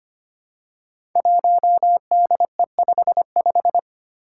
Morse code, heard as a keyed tone, about 25 wpm, 700 Hz. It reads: 1DE55
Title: Five Challenging Letter-Number Combinations 25wpm